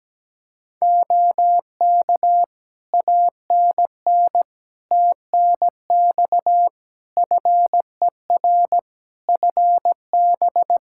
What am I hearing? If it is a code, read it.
Morse code, heard as a keyed tone, about 17 wpm, 700 Hz. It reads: OK ANN TNX FER FB
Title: Practice QSO - Exchange 4 - Ending - 17wpm